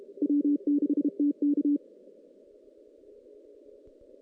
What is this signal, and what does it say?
Morse code, heard as a keyed tone, about 32 wpm, 290 Hz. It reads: W6TK
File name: 2016 JAN CW NAQP / 2016 Jan naqp k5ot - 0-6